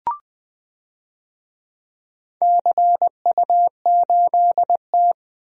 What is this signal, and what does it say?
Morse code, heard as a keyed tone, about 20 wpm, 700 Hz. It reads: CU8T
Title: DXCC-100 Call Signs - 4 Characters - Repeated 3x - 20wpm